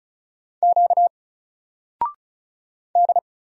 Morse code, heard as a keyed tone, about 35 wpm, 700 Hz. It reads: Q D
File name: Single-Letter-Number-35wpm